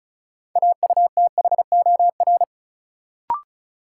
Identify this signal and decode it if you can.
Morse code, heard as a keyed tone, about 36 wpm, 700 Hz. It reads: AUTHOR
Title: Sentences from Top 1200 Words - New Word Drill Alt - 35wpm